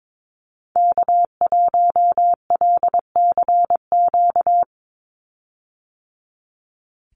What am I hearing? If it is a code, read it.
Morse code, heard as a keyed tone, about 22 wpm, 700 Hz. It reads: K1LCQ
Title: USA Call Signs 22wpm